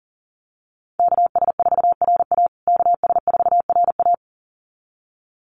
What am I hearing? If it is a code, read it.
Morse code, heard as a keyed tone, about 40 wpm, 700 Hz. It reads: KS4RA KS4RA
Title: USA Call Signs Repeated 2x 40wpm